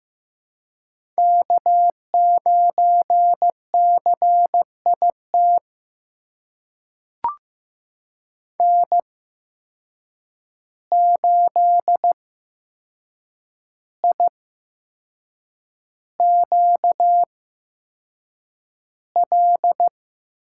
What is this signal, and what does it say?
Morse code, heard as a keyed tone, about 15 wpm, spaced as for 6 wpm, 700 Hz. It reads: K9CIT N8IQL
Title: USA Call Signs - 5 Characters - 8x Character Spacing - 15wpm